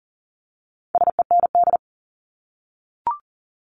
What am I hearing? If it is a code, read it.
Morse code, heard as a keyed tone, about 41 wpm, 700 Hz. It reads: SEND
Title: Multi-letter Phonemes 40wpm